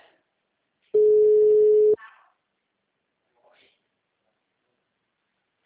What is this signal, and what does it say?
Ringback tone (Europe, 425 Hz), with speech.